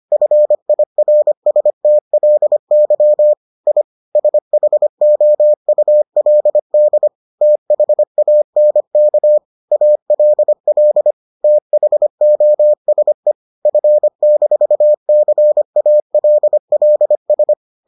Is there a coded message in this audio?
Yes: Morse code from 0.1 to 17.5 s, heard as a keyed tone, about 25 wpm, 600 Hz. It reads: FIRSTLY I SHOULD THANK ALL THOSE F-CALLS